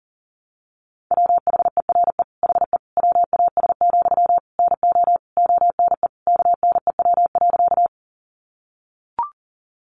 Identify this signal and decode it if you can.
Morse code, heard as a keyed tone, about 40 wpm, 700 Hz. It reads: WHERE HE WAS, NO ONE KNEW.
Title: Sentences from Top 400 Words 40wpm